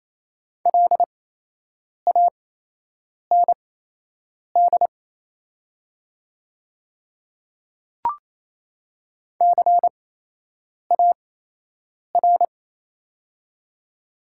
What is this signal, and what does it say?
Morse code, heard as a keyed tone, about 28 wpm, spaced as for 8 wpm, 700 Hz. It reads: LAND CAR
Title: 8x Character Spacing - Top 300 Words - 28wpm